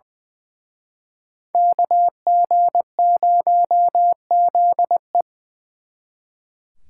Morse code, heard as a keyed tone, about 24 wpm, 700 Hz. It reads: KG0ZE
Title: Speed Racing - USA 2x2 Call Signs 20wpm